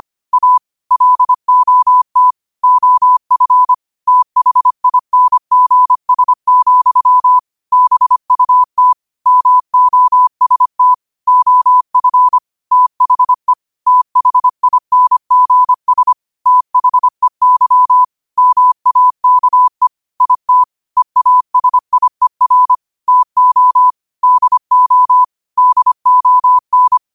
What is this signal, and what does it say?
Morse code, heard as a keyed tone, about 25 wpm, 1000 Hz. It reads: A LOT OF THINGS, BUT MOST OF THE THINGS THEY MAKE IT EASIER TO DO DON
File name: qotdfiles / 25wpm